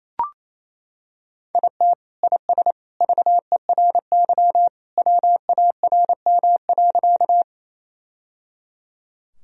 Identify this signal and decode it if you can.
Morse code, heard as a keyed tone, about 28 wpm, 700 Hz. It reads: IT IS VERY WARM.
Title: Sentences from Top 1000 Words - Review Part 2 - 28wpm